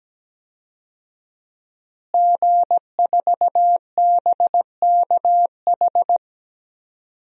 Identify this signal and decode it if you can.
Morse code, heard as a keyed tone, about 17 wpm, 700 Hz. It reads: G4BKH
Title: DXCC-100 Call Signs - 5 Characters - Repeated 3x - 17wpm